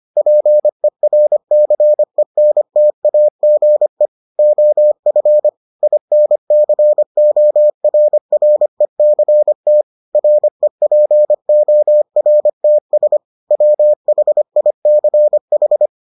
Morse code, heard as a keyed tone, about 25 wpm, 600 Hz. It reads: PERCENTAGE OF INCORRECT REPORTS WHICH